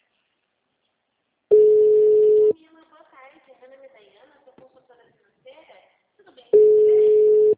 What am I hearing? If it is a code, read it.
Ringback tone (Europe, 425 Hz), with speech.